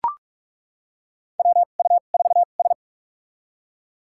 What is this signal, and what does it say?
Morse code, heard as a keyed tone, about 41 wpm, spaced as for 31 wpm, 700 Hz. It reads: WU4S
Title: USA Call Signs - 2x Character Spacing - 45wpm